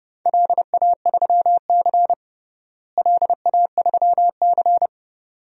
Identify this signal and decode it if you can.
Morse code, heard as a keyed tone, about 30 wpm, 700 Hz. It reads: LA3C LA3C
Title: DXCC-100 Call Signs - 4 Characters - Repeated 2x - 30wpm